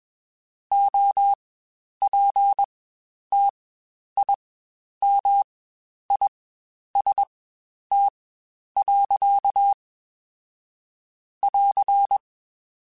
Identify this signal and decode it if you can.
Morse code, heard as a keyed tone, about 21 wpm, spaced as for 10 wpm, 800 Hz. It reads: OPTIMIST. <AR>
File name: Quotes Spaced / Quotes Spaced-mp3s / Optimist sp @20